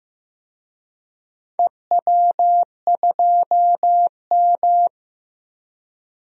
Morse code, heard as a keyed tone, about 15 wpm, 700 Hz. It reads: EW2M
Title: DXCC-100 Call Signs - 4 Characters - Repeated 2x - 15wpm